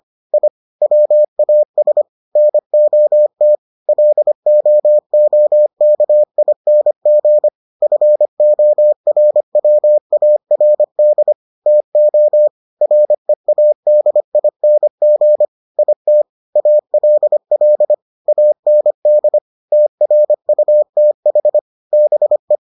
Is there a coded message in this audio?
Yes: Morse code from 0.3 to 22.6 s, heard as a keyed tone, about 25 wpm, 600 Hz. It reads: I WAS NOT LOOKING FORWARD TO READING IT ALL AND TRUTH BE